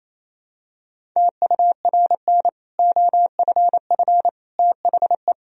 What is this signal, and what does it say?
Morse code, heard as a keyed tone, about 28 wpm, 700 Hz. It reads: TURN OFF THE
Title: Sentences from Top 1000 Words - Review Part 3 - 28wpm